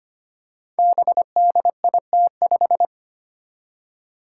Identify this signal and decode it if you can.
Morse code, heard as a keyed tone, about 25 wpm, 700 Hz. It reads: BDIT5
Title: Five Challenging Letter-Number Combinations 25wpm